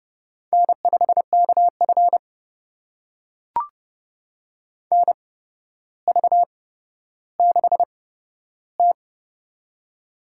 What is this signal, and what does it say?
Morse code, heard as a keyed tone, about 30 wpm, spaced as for 13 wpm, 700 Hz. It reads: N5KF NV6T
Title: USA Call Signs - 4 Characters - 8x Character Spacing - 30wpm